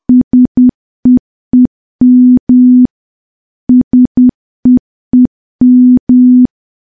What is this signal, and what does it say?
Morse code, heard as a keyed tone, about 10 wpm, 250 Hz. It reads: SEEM SEEM
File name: morse / morse-training-10WPM